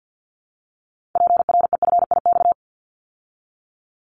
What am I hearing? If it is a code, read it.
Morse code, heard as a keyed tone, about 50 wpm, 700 Hz. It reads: PREFIX
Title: Callsign Prefixes for Top 50 DXCC - 50wpm